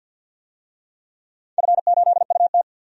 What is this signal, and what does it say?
Morse code, heard as a keyed tone, about 50 wpm, 700 Hz. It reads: F8UT
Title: DXCC-100 Call Signs - 4 Characters - 50wpm